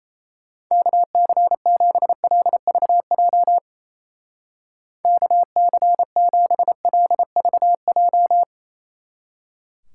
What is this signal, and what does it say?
Morse code, heard as a keyed tone, about 35 wpm, 700 Hz. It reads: KC7LVJ KC7LVJ
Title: Speed Racing - USA 2x3 Call Signs 28wpm